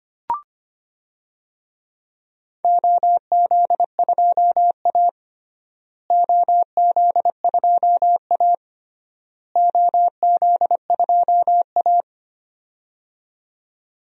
Morse code, heard as a keyed tone, about 25 wpm, 700 Hz. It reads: OZ2A OZ2A OZ2A
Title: DXCC-100 Call Signs - 4 Characters - Repeated 3x - 25wpm